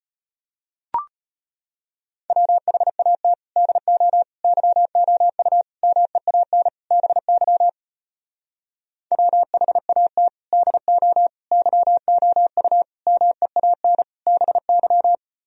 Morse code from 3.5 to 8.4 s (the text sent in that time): DO YOU MEAN BY